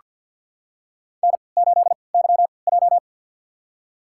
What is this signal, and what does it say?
Morse code, heard as a keyed tone, about 47 wpm, spaced as for 28 wpm, 700 Hz. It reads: N8YJ
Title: USA Call Signs - 4 Characters - 3x Character Spacing - 50wpm